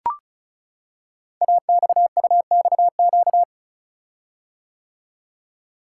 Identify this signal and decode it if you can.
Morse code, heard as a keyed tone, about 36 wpm, 700 Hz. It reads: AXUXQ
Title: Code Groups - 5 Letters - 35wpm